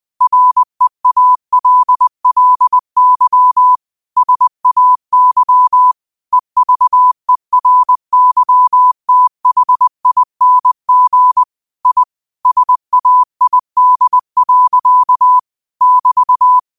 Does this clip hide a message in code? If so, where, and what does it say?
Yes: Morse code from 0.2 to 16.6 s, heard as a keyed tone, about 20 wpm, 1000 Hz. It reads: REALLY SAY EVERYTHING I SAID. <BT>